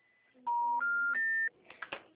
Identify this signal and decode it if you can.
Special information tone.
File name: out- -2019- - - .67467